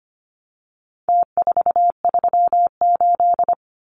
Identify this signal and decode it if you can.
Morse code, heard as a keyed tone, about 25 wpm, 700 Hz. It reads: T438
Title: Numbers - Contest Style - 4-Digit 25wpm